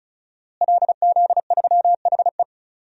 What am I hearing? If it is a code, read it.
Morse code, heard as a keyed tone, about 35 wpm, 700 Hz. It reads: LZ3HE
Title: DXCC-100 Call Signs - 5 Characters - Repeated 2x - 35wpm